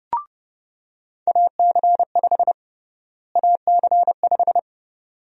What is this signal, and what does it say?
Morse code, heard as a keyed tone, about 31 wpm, 700 Hz. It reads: AC5 AC5
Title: USA Call Sign Prefixes - Repeated 2x - 30wpm